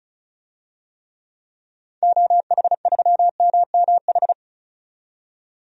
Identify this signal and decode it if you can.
Morse code, heard as a keyed tone, about 36 wpm, 700 Hz. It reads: OH3MMH
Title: DXCC-100 Call Signs - 6 Characters - 35wpm